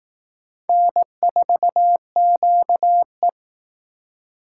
Morse code, heard as a keyed tone, about 19 wpm, 700 Hz. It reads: N4QE
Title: Speed Racing - USA 1x2 Call Signs 15wpm